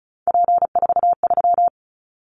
Morse code, heard as a keyed tone, about 34 wpm, 700 Hz. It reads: P43
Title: Callsign Prefixes for Top 100 DXCC - 35wpm